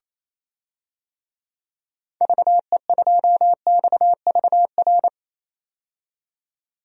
Morse code, heard as a keyed tone, about 27 wpm, 700 Hz. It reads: VE2XVR
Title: DXCC-100 Call Signs - 6 Characters - 28wpm